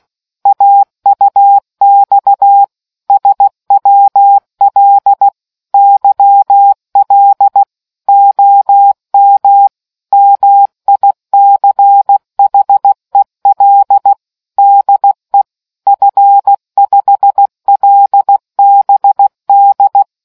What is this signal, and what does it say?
Morse code, heard as a keyed tone, about 16 wpm, 800 Hz. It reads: AUX SWL YL OM MICHEL DE F5LBD